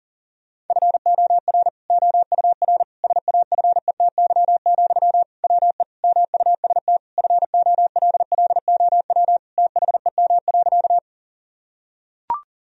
Morse code, heard as a keyed tone, about 40 wpm, 700 Hz. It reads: FOR OUR SAFETY, WE MUST FOLLOW THEM.
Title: Sentences from Top 1000 Words - Review Part 4 - 40wpm